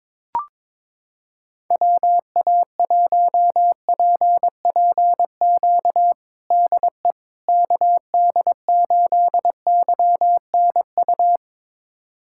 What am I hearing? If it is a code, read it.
Morse code, heard as a keyed tone, about 22 wpm, 700 Hz. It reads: WA1PPQ DE KD8YNU
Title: USA Callsign DE Callsign - 6 Characters - 22wpm